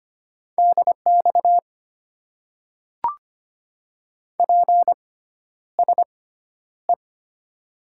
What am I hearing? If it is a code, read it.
Morse code, heard as a keyed tone, about 25 wpm, spaced as for 11 wpm, 700 Hz. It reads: DX PSE
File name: Top 90 QSO Elements - 6x Character Spacing - 25wpm